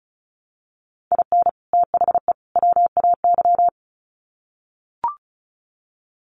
Morse code, heard as a keyed tone, about 34 wpm, 700 Hz. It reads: IN THE WAY
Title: Sets of 3 Words 35wpm